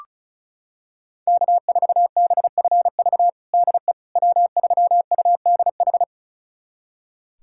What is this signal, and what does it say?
Morse code, heard as a keyed tone, about 36 wpm, 690 Hz. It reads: K4BFV DE W3UDH
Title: USA Callsign DE Callsign - 5 Characters - 35wpm